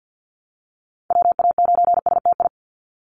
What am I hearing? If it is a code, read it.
Morse code, heard as a keyed tone, about 50 wpm, 700 Hz. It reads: WA9STI
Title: NA SOTA Activator Callsigns 50wpm